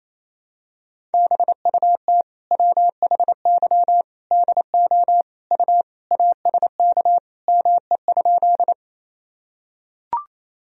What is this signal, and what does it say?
Morse code, heard as a keyed tone, about 28 wpm, 700 Hz. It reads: BUT WHY DO U ASK ME?
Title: Sentences from Top 200 Words and Abbreviations 28wpm